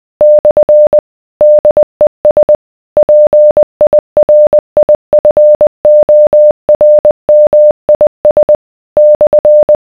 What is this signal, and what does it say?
Morse code, heard as a keyed tone, about 20 wpm, 600 Hz. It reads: / DES PIRIFORMIS /